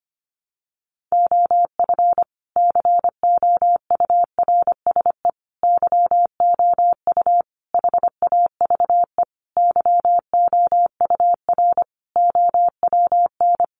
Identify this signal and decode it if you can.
Morse code, heard as a keyed tone, about 25 wpm, 700 Hz. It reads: OF COURSE YOU HAVE YOUR OWN
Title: Sentences from Top 1200 Words 25wpm